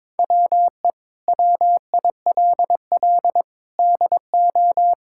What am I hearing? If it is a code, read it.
Morse code, heard as a keyed tone, about 22 wpm, 700 Hz. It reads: WE WILL DO
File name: Sentences with 7-Word Phrases - 22wpm